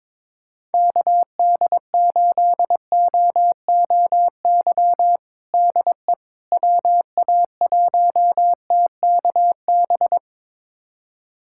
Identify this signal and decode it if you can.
Morse code, heard as a keyed tone, about 22 wpm, 700 Hz. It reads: KD8OOY DE WA1TKB
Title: USA Callsign DE Callsign - 6 Characters - 22wpm